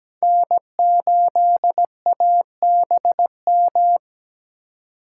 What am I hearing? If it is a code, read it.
Morse code, heard as a keyed tone, about 17 wpm, 700 Hz. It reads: N8ABM
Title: USA Call Signs - 5 Characters - 6x Character Spacing - 17wpm